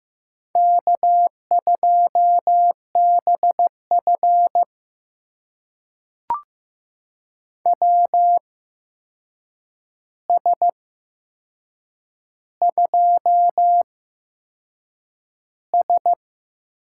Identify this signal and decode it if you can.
Morse code, heard as a keyed tone, about 15 wpm, spaced as for 6 wpm, 700 Hz. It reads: K2BF WS2S
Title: USA Call Signs - 4 Characters - 8x Character Spacing - 15wpm